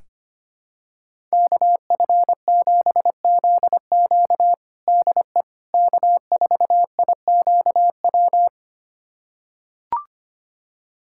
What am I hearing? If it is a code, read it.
Morse code, heard as a keyed tone, about 25 wpm, 700 Hz. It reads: KF7ZQ DE K4IQW
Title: USA Callsign DE Callsign - 5 Characters - 25wpm